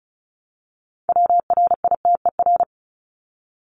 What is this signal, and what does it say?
Morse code, heard as a keyed tone, about 34 wpm, 700 Hz. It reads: WRITER
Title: Top 1751-2000 Words 35wpm